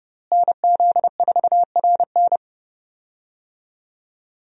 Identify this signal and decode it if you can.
Morse code, heard as a keyed tone, about 30 wpm, 700 Hz. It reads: NZ4RN
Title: USA Call Signs - 5 Characters - Repeated 2x - 30wpm